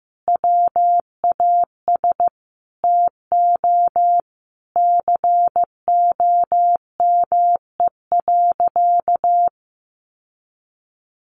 Morse code, heard as a keyed tone, about 15 wpm, 700 Hz. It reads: WAS TO COME.
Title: Sentences from Top 700 Words 15wpm